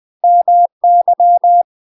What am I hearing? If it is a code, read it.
Morse code, heard as a keyed tone, about 20 wpm, 700 Hz. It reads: MY
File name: UK MASTER QSOs TRAINING / 20WPM / UK MASTER QSOs 0014